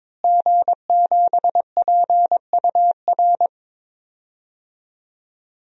Morse code, heard as a keyed tone, about 22 wpm, 700 Hz. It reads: G7PUR
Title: DXCC-100 Call Signs - 5 Characters - Repeated 3x - 22wpm